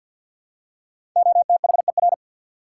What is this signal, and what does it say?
Morse code, heard as a keyed tone, about 52 wpm, 700 Hz. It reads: OTHER
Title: Five Letter Words - 50wpm